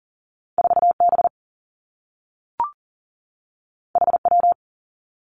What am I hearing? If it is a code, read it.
Morse code, heard as a keyed tone, about 40 wpm, 700 Hz. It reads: 4 B H W
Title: Two Challenging Letter-Number Combinations 40wpm